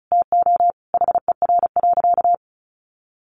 Morse code, heard as a keyed tone, about 34 wpm, 700 Hz. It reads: TO HER.
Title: Sentences from Top 1100 Words 35wpm